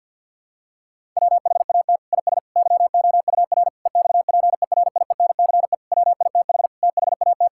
Morse code, heard as a keyed tone, about 51 wpm, 700 Hz. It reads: WHAT IS YOUR EXPERIENCE WITH THAT
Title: Sentences from Top 1000 Words - Review Part 5 - 50wpm